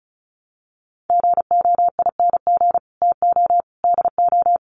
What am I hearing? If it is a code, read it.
Morse code, heard as a keyed tone, about 34 wpm, 700 Hz. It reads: GOING TO DO
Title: Sets of 3 Words 35wpm